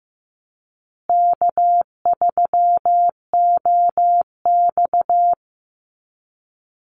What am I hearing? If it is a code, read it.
Morse code, heard as a keyed tone, about 15 wpm, 700 Hz. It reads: K3OX
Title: Call Signs Repeated 2x - 15wpm